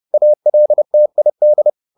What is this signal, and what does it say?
Morse code, heard as a keyed tone, about 30 wpm, 600 Hz. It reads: ALTID
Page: Ferie 30 wpm